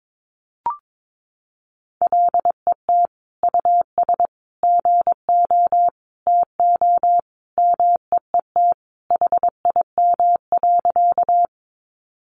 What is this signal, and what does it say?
Morse code, heard as a keyed tone, about 22 wpm, 700 Hz. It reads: LET US GO TO MEET HIM.
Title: Sentences from Top 1100 Words 22wpm